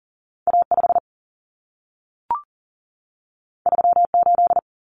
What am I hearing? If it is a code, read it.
Morse code, heard as a keyed tone, about 40 wpm, 700 Hz. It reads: A5 38
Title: Common Cut Numbers - 2-Digit 40wpm